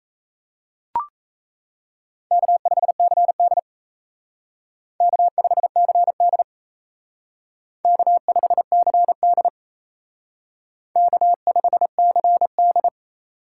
Morse code, heard as a keyed tone, about 35 wpm, 700 Hz. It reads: K5CD K5CD K5CD K5CD
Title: Speed Racing - USA 1x2 Call Signs 28wpm